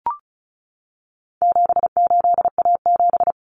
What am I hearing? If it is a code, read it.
Morse code, heard as a keyed tone, about 34 wpm, 700 Hz. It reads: 78A7
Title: Common Cut Numbers - 4-Digit 35wpm